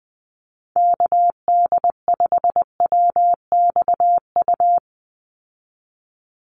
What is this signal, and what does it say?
Morse code, heard as a keyed tone, about 20 wpm, 700 Hz. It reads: KD5WXU
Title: USA Call Signs 20wpm